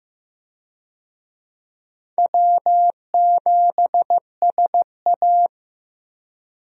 Morse code, heard as a keyed tone, about 16 wpm, 700 Hz. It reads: W7SA
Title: USA Call Signs - 4 Characters - Repeated 2x - 15wpm